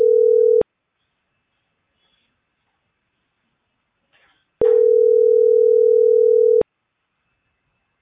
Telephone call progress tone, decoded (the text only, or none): ringback tone